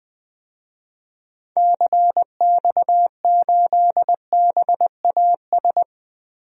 Morse code, heard as a keyed tone, about 20 wpm, 700 Hz. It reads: CX8BAS
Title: DXCC-100 Call Signs - 6 Characters - 20wpm